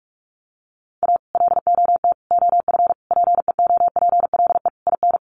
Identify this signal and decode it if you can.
Morse code, heard as a keyed tone, about 44 wpm, 700 Hz. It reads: A LOT OF PEOPLE IN
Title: Sets of 5 Words 45wpm